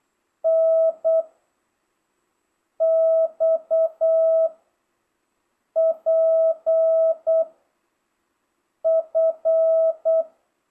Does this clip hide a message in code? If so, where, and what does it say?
Yes: Morse code from 0.5 to 10.2 s, heard as a keyed tone, about 8 wpm, 660 Hz. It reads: N X P F